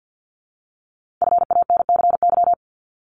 Morse code, heard as a keyed tone, about 50 wpm, 690 Hz. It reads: FANCY
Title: Multi-letter Phonemes 50wpm